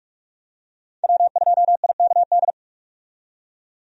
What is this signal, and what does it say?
Morse code, heard as a keyed tone, about 43 wpm, 700 Hz. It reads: W2IKD
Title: USA Call Signs - 5 Characters - 45wpm